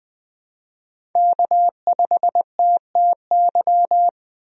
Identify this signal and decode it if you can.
Morse code, heard as a keyed tone, about 20 wpm, 700 Hz. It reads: K5TTY
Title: USA Call Signs - 5 Characters - Repeated 3x - 20wpm